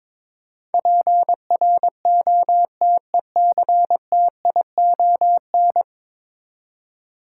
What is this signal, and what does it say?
Morse code, heard as a keyed tone, about 21 wpm, 700 Hz. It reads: PROTECTION
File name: practice / 22wpm / 10 Letter Words - First 3 Letters Given - 22wpm